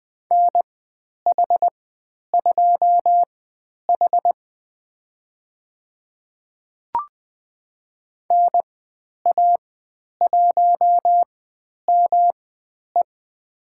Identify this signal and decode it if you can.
Morse code, heard as a keyed tone, about 20 wpm, spaced as for 10 wpm, 700 Hz. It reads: NH2H NA1ME
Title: 109 Lesson 18g - Letter m - Call Signs 20wpm